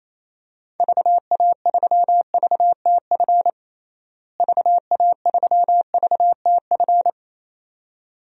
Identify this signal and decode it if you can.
Morse code, heard as a keyed tone, about 27 wpm, 700 Hz. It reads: VA3VTF VA3VTF
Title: DXCC-100 Call Signs - 6 Characters - Repeated 2x - 28wpm